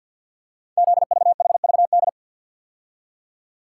Morse code, heard as a keyed tone, about 48 wpm, 690 Hz. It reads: ZVHVD
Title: Code Groups - 5 Letters - 50wpm